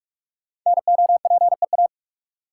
Morse code, heard as a keyed tone, about 44 wpm, 700 Hz. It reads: NOPEA
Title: Code Groups - 5 Letters - 45wpm